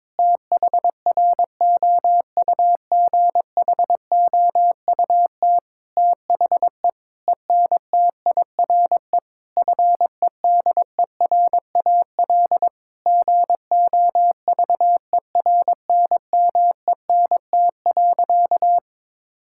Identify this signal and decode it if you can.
Morse code, heard as a keyed tone, about 22 wpm, 700 Hz. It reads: THROUGHOUT THE ENTIRE FEDERAL GOVERNMENT.
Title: Sentences from Top 1000 Words - Review Part 5 - 22wpm